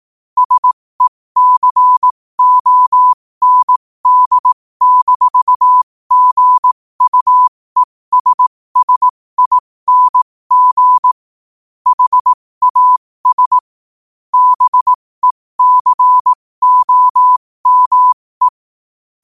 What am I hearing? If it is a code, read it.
Morse code, heard as a keyed tone, about 18 wpm, spaced as for 15 wpm, 1000 Hz. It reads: SECOND-GUESSING HAS BECOME
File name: qotdfiles / 15wpm